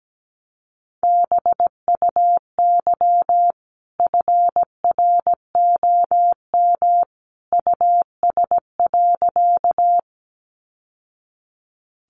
Morse code, heard as a keyed tone, about 17 wpm, 700 Hz. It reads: BUY FROM US.
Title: Sentences from Top 700 Words 17wpm